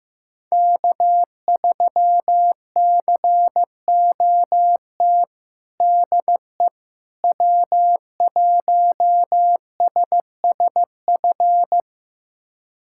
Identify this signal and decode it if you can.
Morse code, heard as a keyed tone, about 14 wpm, 700 Hz. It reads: K3COT DE W1SSF
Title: USA Callsign DE Callsign 15wpm